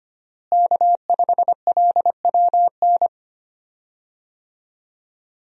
Morse code, heard as a keyed tone, about 25 wpm, 700 Hz. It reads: K5LWN
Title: USA Call Signs - 5 Characters - Repeated 3x - 25wpm